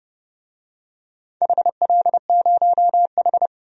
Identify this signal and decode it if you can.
Morse code, heard as a keyed tone, about 30 wpm, 700 Hz. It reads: HL0H
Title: DXCC-100 Call Signs - 4 Characters - 30wpm